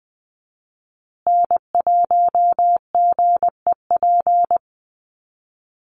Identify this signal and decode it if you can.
Morse code, heard as a keyed tone, about 20 wpm, 700 Hz. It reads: N1GEP
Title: USA Call Signs Repeated 3x 20wpm